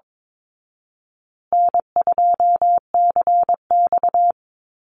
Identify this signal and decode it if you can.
Morse code, heard as a keyed tone, about 22 wpm, 700 Hz. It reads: N2CX
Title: Call Signs Repeated 3x - 22wpm